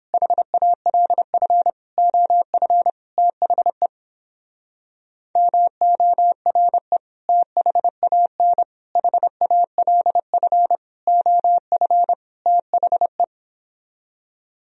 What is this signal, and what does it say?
Morse code, heard as a keyed tone, about 29 wpm, 700 Hz. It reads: HALF OF THE MORE THAN HALF OF THE